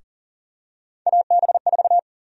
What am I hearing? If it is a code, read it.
Morse code, heard as a keyed tone, about 40 wpm, 700 Hz. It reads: AB4
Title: USA Call Sign Prefixes - Repeated 2x - 40wpm